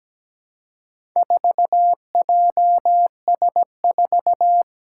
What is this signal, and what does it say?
Morse code, heard as a keyed tone, about 17 wpm, 700 Hz. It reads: 4JS4
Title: Four Challenging Letter-Number Combinations 17wpm